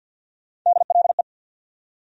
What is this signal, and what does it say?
Morse code, heard as a keyed tone, about 47 wpm, 700 Hz. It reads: DLE
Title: USA Call Sign Postfixes - 3 Letters 50wpm